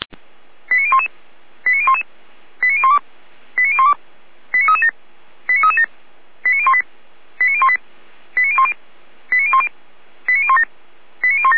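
Five-tone selective call (ZVEI), heard as quick runs of five tones, tones of about 70 ms. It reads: 89011 89011 89012 89012 89337 89337 89018 89018 89010 89010 89017 89017
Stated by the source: ZVEI five-tone selective call received on an SDR.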